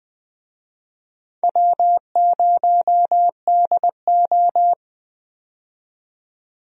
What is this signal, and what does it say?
Morse code, heard as a keyed tone, about 20 wpm, 700 Hz. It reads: W0DO
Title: USA Call Signs - 4 Characters - 20wpm